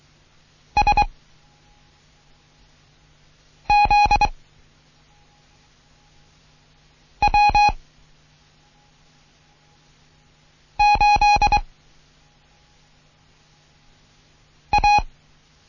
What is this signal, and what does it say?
Morse code, heard as a keyed tone, about 23 wpm, 830 Hz. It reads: S Z W 8 A
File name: Morsen / Kurs A 16